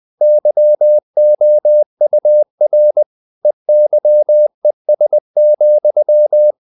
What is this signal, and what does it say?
Morse code, heard as a keyed tone, about 20 wpm, 600 Hz. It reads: YOUR EYES,
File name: the old man and the sea / the old man and the sea part 1 20wpm